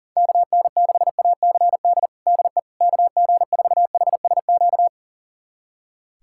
Morse code, heard as a keyed tone, about 40 wpm, 700 Hz. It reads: KN6ACD DE KG4HSQ
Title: USA Callsign DE Callsign 40wpm